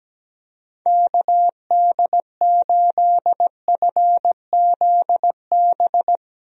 Morse code, heard as a keyed tone, about 17 wpm, 700 Hz. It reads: KD8FZB
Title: USA Call Signs - 6 Characters - 17wpm